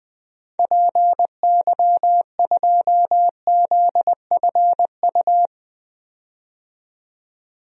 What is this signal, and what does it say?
Morse code, heard as a keyed tone, about 20 wpm, 700 Hz. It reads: PY2ZFU
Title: DXCC-100 Call Signs - 6 Characters - 20wpm